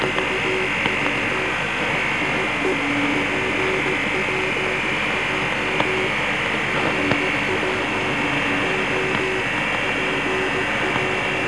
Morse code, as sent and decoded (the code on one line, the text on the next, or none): none